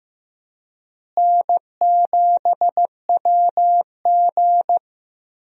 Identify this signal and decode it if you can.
Morse code, heard as a keyed tone, about 15 wpm, 700 Hz. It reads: N7WG
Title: USA Call Signs - 4 Characters - Repeated 2x - 15wpm